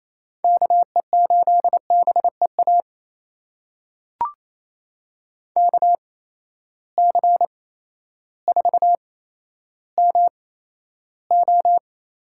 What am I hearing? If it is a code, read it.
Morse code, heard as a keyed tone, about 28 wpm, spaced as for 13 wpm, 700 Hz. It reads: KE8BEA KC4MO
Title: USA Call Signs - 6 Characters - 8x Character Spacing - 28wpm